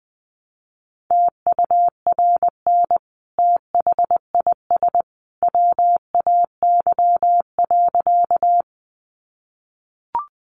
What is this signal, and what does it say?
Morse code, heard as a keyed tone, about 20 wpm, 700 Hz. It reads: TURN THIS WAY.
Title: Sentences from Top 200 Words 20wpm